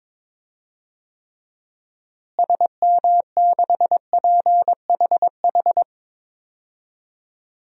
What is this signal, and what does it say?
Morse code, heard as a keyed tone, about 22 wpm, 700 Hz. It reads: SM6PHH
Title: DXCC-100 Call Signs - 6 Characters - Repeated 2x - 22wpm